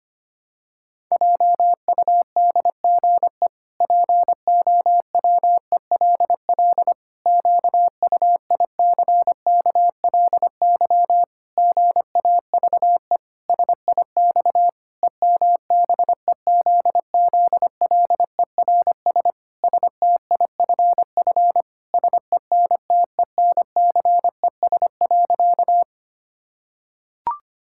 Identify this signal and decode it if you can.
Morse code, heard as a keyed tone, about 25 wpm, 700 Hz. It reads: JUDGE POWELL QUICKLY GAVE SIX EMBEZZLERS STIFF SENTENCES.
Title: Pangram Challenge v2 - 25wpm